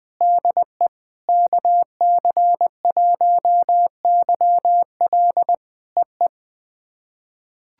Morse code, heard as a keyed tone, about 20 wpm, 700 Hz. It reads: DE KC1YL EE
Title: POTA QSO Practice - 20wpm